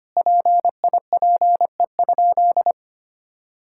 Morse code, heard as a keyed tone, about 25 wpm, 700 Hz. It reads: PIPE?
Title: Sentences from Top 1600 Words 25wpm